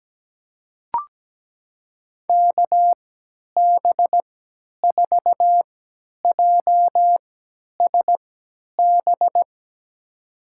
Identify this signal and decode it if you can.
Morse code, heard as a keyed tone, about 17 wpm, spaced as for 10 wpm, 700 Hz. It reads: KB4JSB
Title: USA Call Signs - 6 Characters - 3x Character Spacing - 17wpm